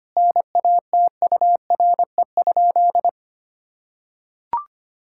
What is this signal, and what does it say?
Morse code, heard as a keyed tone, about 25 wpm, 700 Hz. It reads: NATURE?
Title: Sentences from Top 1000 Words - Review Part 5 - 25wpm